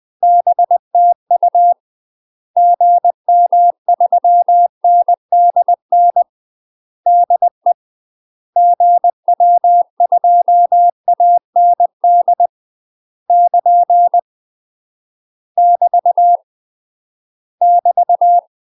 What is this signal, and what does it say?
Morse code, heard as a keyed tone, about 20 wpm, 700 Hz. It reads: BTU GM3NDN DE GW2AND <KN> <BT> <BT>
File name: UK MASTER QSOs TRAINING / 20WPM / UK MASTER QSOs 0002